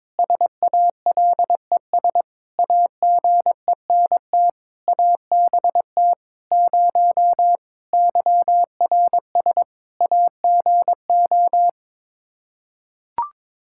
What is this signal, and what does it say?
Morse code, heard as a keyed tone, about 22 wpm, 700 Hz. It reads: SALES AGENT ABT 0 YRS AGO
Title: Practice QSO - Exchange 3 - More Info - 22wpm